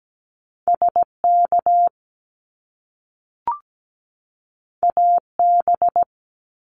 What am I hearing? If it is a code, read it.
Morse code, heard as a keyed tone, about 17 wpm, 700 Hz. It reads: SK AB
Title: Canadian Provinces and Territories 17wpm